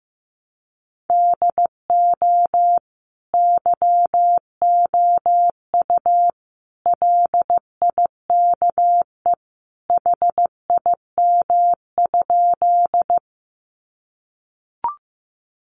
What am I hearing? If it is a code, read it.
Morse code, heard as a keyed tone, about 15 wpm, 700 Hz. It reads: DO YOU LIKE HIM?
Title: Sentences from Top 100 Words 15wpm